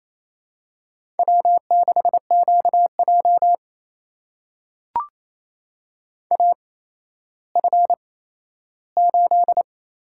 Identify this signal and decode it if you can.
Morse code, heard as a keyed tone, about 27 wpm, spaced as for 13 wpm, 700 Hz. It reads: W6QJ AF8
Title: USA Call Signs - 4 Characters - 8x Character Spacing - 28wpm